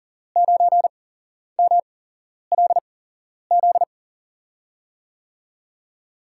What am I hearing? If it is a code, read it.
Morse code, heard as a keyed tone, about 39 wpm, spaced as for 11 wpm, 700 Hz. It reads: 9MLZ
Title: USA Call Signs - 8x Character Spacing - 40wpm